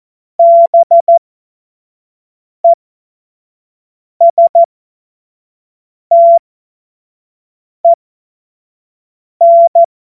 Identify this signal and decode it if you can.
Morse code, heard as a keyed tone, about 14 wpm, spaced as for 5 wpm, 680 Hz. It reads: BESTEN
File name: Morsekurs / Witze / Witz 05 Dezember 2023 25 F70